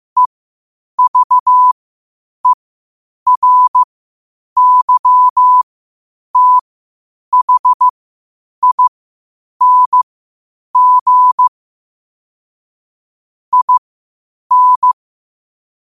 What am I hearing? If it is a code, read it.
Morse code, heard as a keyed tone, about 15 wpm, spaced as for 8 wpm, 1000 Hz. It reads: EVERYTHING IN
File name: qotdfiles / 8wpm